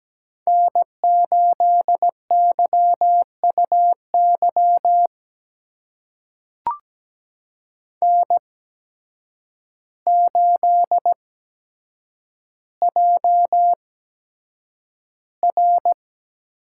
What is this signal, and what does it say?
Morse code, heard as a keyed tone, about 17 wpm, spaced as for 8 wpm, 700 Hz. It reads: N8YUY N8JR
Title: USA Call Signs - 8x Character Spacing - 17wpm